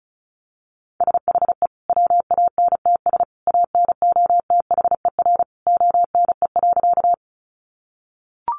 Morse code, heard as a keyed tone, about 34 wpm, 700 Hz. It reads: SHE WANTS ANOTHER ONE.
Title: Easy Phrases 35wpm